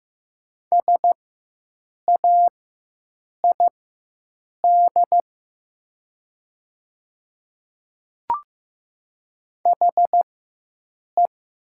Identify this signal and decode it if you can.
Morse code, heard as a keyed tone, about 15 wpm, spaced as for 7 wpm, 700 Hz. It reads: SAID HE
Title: 4x Character Spacing - Top 100 Words - 15wpm